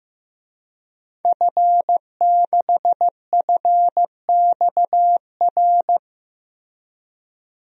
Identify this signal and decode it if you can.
Morse code, heard as a keyed tone, about 15 wpm, 700 Hz. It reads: F6FXR
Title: DXCC-100 Call Signs - 5 Characters - Repeated 2x - 15wpm